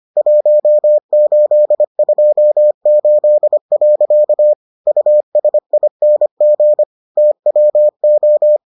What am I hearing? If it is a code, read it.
Morse code, heard as a keyed tone, about 25 wpm, 600 Hz. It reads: 1828. USING TWO